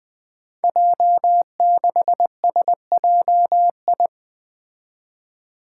Morse code, heard as a keyed tone, about 20 wpm, 700 Hz. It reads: J6SJI
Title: Five Challenging Letter-Number Combinations 20wpm